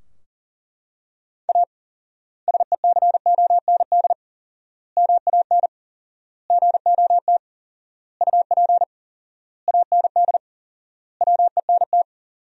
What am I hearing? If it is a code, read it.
Morse code, heard as a keyed tone, about 37 wpm, 700 Hz. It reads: A SECOND MAN GOT UP AND WENT
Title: 4x Word-Spacing - Sentences from Top 300 Words - 40wpm